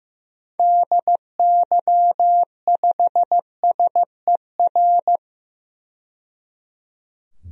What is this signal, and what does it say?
Morse code, heard as a keyed tone, about 15 wpm, 690 Hz. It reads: DY5SER
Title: DXCC-100 Call Signs - 6 Characters - 15wpm